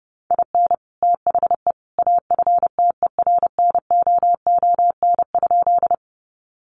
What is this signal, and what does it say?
Morse code, heard as a keyed tone, about 30 wpm, 700 Hz. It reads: IN THE AFTERNOON?
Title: Sentences from Top 1500 Words 30wpm